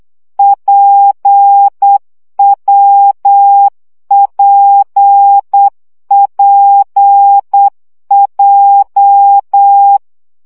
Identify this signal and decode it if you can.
Morse code, heard as a keyed tone, about 8 wpm, 800 Hz. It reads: PWPPJ
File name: CW gyakorlas w p j 7 baud